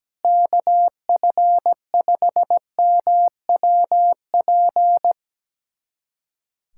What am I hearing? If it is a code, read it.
Morse code, heard as a keyed tone, about 17 wpm, 700 Hz. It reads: KF5MWP
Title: USA Callsign DE Callsign 17wpm